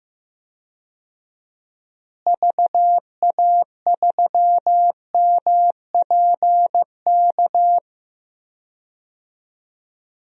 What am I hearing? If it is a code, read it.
Morse code, heard as a keyed tone, about 15 wpm, 700 Hz. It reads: VA3MPK
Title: DXCC-100 Call Signs - 6 Characters - Repeated 2x - 15wpm